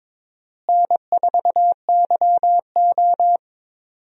Morse code, heard as a keyed tone, about 22 wpm, 700 Hz. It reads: N4YO
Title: USA Call Signs - 4 Characters - Repeated 3x - 22wpm